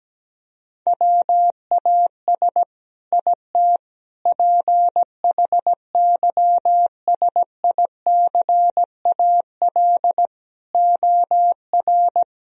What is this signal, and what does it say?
Morse code, heard as a keyed tone, about 17 wpm, 700 Hz. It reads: WAS IT PHYSICAL OR
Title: Sentences from Top 1000 Words - Review Part 4 - 17wpm